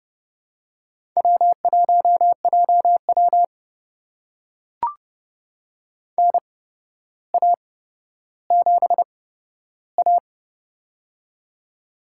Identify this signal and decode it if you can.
Morse code, heard as a keyed tone, about 30 wpm, spaced as for 13 wpm, 700 Hz. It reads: W1JW NA7A
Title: USA Call Signs - 4 Characters - 8x Character Spacing - 30wpm